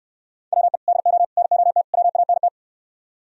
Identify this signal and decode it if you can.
Morse code, heard as a keyed tone, about 68 wpm, 700 Hz. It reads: WE DO NOT WANT